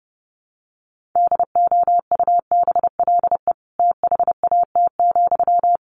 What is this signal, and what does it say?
Morse code, heard as a keyed tone, about 28 wpm, 700 Hz. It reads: DOUBLE THAT,
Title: Sentences from Top 1400 Words 30wpm